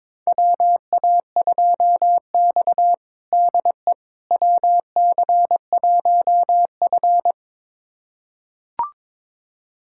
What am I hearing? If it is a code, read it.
Morse code, heard as a keyed tone, about 22 wpm, 700 Hz. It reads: WA2X DE WC1F
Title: USA Callsign DE Callsign - 4 Characters - 22wpm